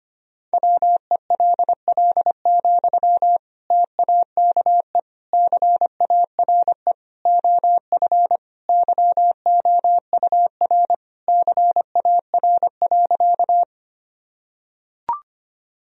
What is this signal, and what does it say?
Morse code, heard as a keyed tone, about 25 wpm, 700 Hz. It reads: WELL, TAKE CARE OF YOUR CAR.
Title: Sentences from Top 1000 Words - Review Part 1 - 25wpm